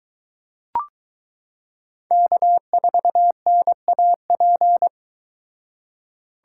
Morse code, heard as a keyed tone, about 22 wpm, 700 Hz. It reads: K4NAP